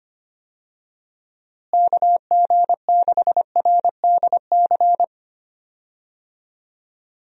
Morse code, heard as a keyed tone, about 25 wpm, 700 Hz. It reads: KG6RDC
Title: USA Call Signs - 6 Characters - Repeated 2x - 25wpm